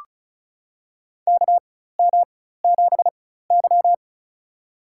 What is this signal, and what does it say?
Morse code, heard as a keyed tone, about 36 wpm, spaced as for 16 wpm, 700 Hz. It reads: KM7Y
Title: USA Call Signs - 4 Characters - 4x Character Spacing - 35wpm